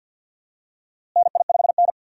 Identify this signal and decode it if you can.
Morse code, heard as a keyed tone, about 48 wpm, 700 Hz. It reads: NI5N
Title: USA Call Signs - 4 Characters - Repeated 2x - 50wpm